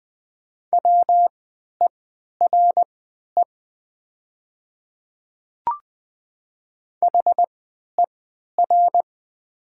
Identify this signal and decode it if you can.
Morse code, heard as a keyed tone, about 20 wpm, spaced as for 11 wpm, 700 Hz. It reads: WERE HER
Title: 3x Character Spacing - Top 100 Words - 20wpm